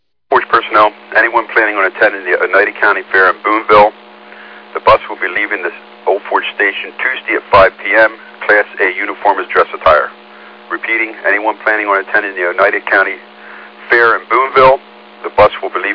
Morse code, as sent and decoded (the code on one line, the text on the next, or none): none